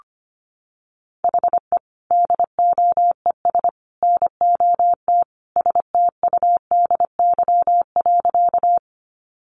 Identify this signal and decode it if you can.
Morse code, heard as a keyed tone, about 25 wpm, 700 Hz. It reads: HE DOES NOT STUDY.
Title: Additional Easy Phrases 25wpm